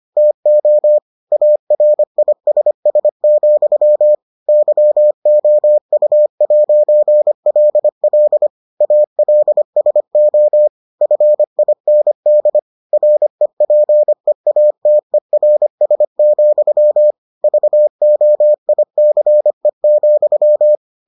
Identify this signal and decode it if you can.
Morse code, heard as a keyed tone, about 25 wpm, 600 Hz. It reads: TO ARISS, YOU'LL ALSO FIND REPEATERS, VOICE,